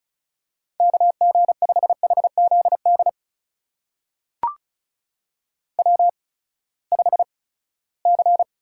Morse code, heard as a keyed tone, about 36 wpm, spaced as for 20 wpm, 700 Hz. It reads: KG5HZD W5C
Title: USA Call Signs - 8x Character Spacing - 35wpm